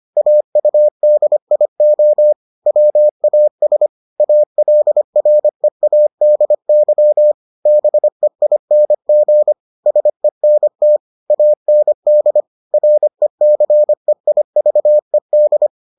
Morse code, heard as a keyed tone, about 25 wpm, 600 Hz. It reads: AUDIO WAS ALREADY BEING SENT AND RECEIVED